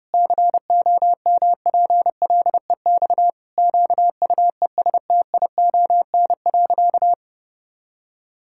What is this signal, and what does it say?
Morse code, heard as a keyed tone, about 32 wpm, 700 Hz. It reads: COMPLEX QUESTION.
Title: Sentences from Top 1000 Words - Review Part 6 - 30wpm